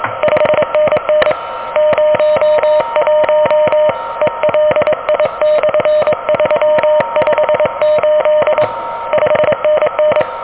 Morse code, heard as a keyed tone, about 22 wpm, 600 Hz, with noise. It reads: 5NN 01 ELI/358 5NN
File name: OK QRP 2023 / OK QRP 2023